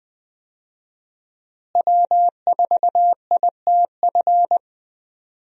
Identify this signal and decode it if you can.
Morse code, heard as a keyed tone, about 20 wpm, 700 Hz. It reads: W4ITF
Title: USA Call Signs - 5 Characters - Repeated 3x - 20wpm